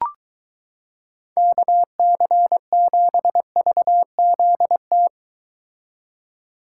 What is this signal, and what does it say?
Morse code, heard as a keyed tone, about 23 wpm, 700 Hz. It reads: KC7VZT